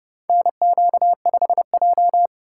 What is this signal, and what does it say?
Morse code, heard as a keyed tone, about 30 wpm, 700 Hz. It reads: NQ5J
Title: USA Call Signs - 4x Character Spacing - 30wpm